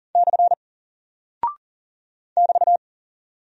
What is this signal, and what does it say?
Morse code, heard as a keyed tone, about 38 wpm, 700 Hz. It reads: / <BT>
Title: Single Character (Letter:Number:Punctuation:Prosign) Alt 40wpm